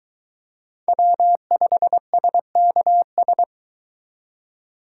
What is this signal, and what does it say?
Morse code, heard as a keyed tone, about 23 wpm, 700 Hz. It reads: W5SKS